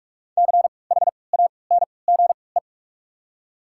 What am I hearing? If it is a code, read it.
Morse code, heard as a keyed tone, about 46 wpm, spaced as for 26 wpm, 700 Hz. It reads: CHANGE
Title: 3x Character Spacing - Top 200 Words - 45wpm